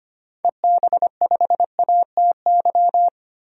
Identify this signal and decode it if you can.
Morse code, heard as a keyed tone, about 25 wpm, 700 Hz. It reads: EB5ATY
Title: DXCC-100 Call Signs - 6 Characters - 25wpm